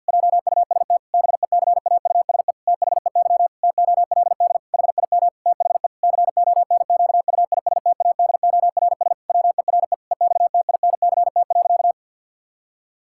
Morse code, heard as a keyed tone, about 50 wpm, 700 Hz. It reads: JUST BECAUSE THEY TOLD HIM THE CONQUISTADORS WERE EXTINCT.
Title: Pangram Challenge v2 - 50wpm